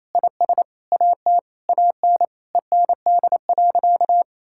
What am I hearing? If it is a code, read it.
Morse code, heard as a keyed tone, about 27 wpm, 700 Hz. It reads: IS AT AN END.
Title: Sentences from Top 1600 Words 28wpm